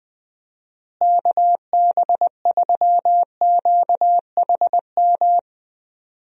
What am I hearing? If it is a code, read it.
Morse code, heard as a keyed tone, about 20 wpm, 700 Hz. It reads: KB3QHM
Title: USA Call Signs - 6 Characters - Repeated 3x - 20wpm